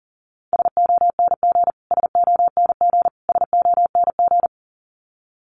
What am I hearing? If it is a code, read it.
Morse code, heard as a keyed tone, about 40 wpm, 700 Hz. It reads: SONG SONG SONG
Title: Top 500 Words Repeated 40wpm